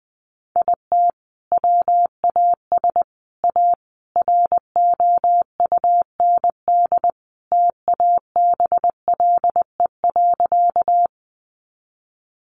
Morse code, heard as a keyed tone, about 20 wpm, 700 Hz. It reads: IT WAS A ROUND TABLE.
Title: Sentences from Top 400 Words 20wpm